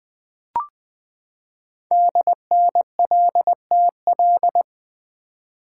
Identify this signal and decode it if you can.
Morse code, heard as a keyed tone, about 20 wpm, 700 Hz. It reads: DNLTL
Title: Code Groups - 5 Letters - 20wpm